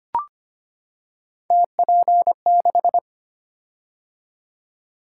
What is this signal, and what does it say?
Morse code, heard as a keyed tone, about 25 wpm, 700 Hz. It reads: TP6
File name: Three Letter-Number Combinations - 25wpm